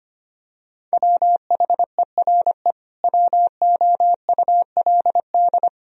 Morse code, heard as a keyed tone, about 25 wpm, 700 Hz. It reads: WHERE WOULD
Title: Sentences from Top 1000 Words - Review Part 3 - 25wpm